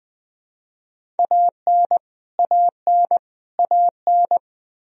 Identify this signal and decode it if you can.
Morse code, heard as a keyed tone, about 20 wpm, 700 Hz. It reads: AN AN AN
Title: Top 100 Words Repeated 20wpm